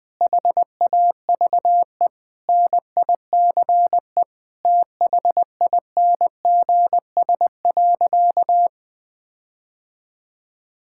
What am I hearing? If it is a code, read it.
Morse code, heard as a keyed tone, about 21 wpm, 700 Hz. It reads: HAVE NICE THINGS.
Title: Sentences with 8-Word Phrases - 20wpm